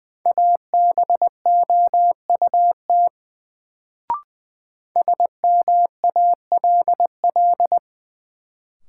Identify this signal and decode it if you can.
Morse code, heard as a keyed tone, about 20 wpm, 700 Hz. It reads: ABOUT SMALL
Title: Sets of 2 Words - New Word Drill - 20wpm